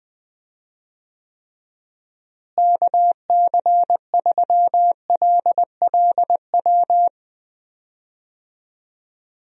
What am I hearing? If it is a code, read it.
Morse code, heard as a keyed tone, about 20 wpm, 700 Hz. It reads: KC3LLW
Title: USA Call Signs - 6 Characters - Repeated 3x - 20wpm